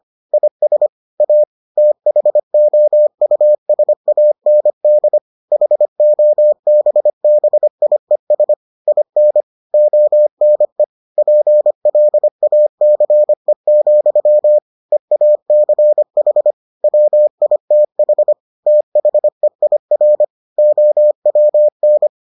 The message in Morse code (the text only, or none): IS A THOUSAND HOBBIES IN ONE PLACE, EACH WITH THEIR OWN